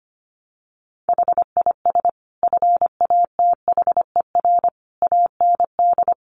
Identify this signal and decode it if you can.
Morse code, heard as a keyed tone, about 25 wpm, 700 Hz. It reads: HIS FATHER AND
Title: Sentences from Top 1500 Words 25wpm